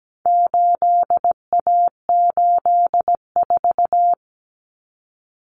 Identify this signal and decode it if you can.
Morse code, heard as a keyed tone, about 17 wpm, 700 Hz. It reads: 8A84
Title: Common Cut Numbers - 4-Digit 17wpm